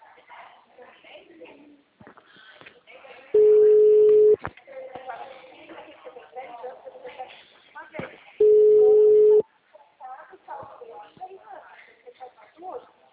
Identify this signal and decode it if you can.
Ringback tone (Europe, 425 Hz), with speech.